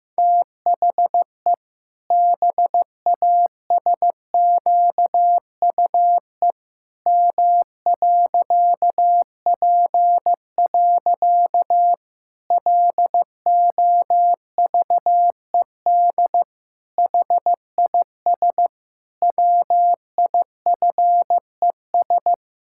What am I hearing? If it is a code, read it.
Morse code, heard as a keyed tone, about 15 wpm, 700 Hz. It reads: THE BASQUE M.P. LOVED HIS WIFES
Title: Pangram Challenge v2 - 15wpm